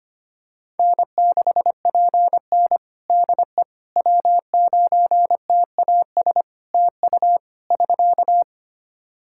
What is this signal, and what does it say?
Morse code, heard as a keyed tone, about 25 wpm, 700 Hz. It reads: N6PN DE W9TAS TU <SK>
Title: Practice QSO - Exchange 4 - Ending - 25wpm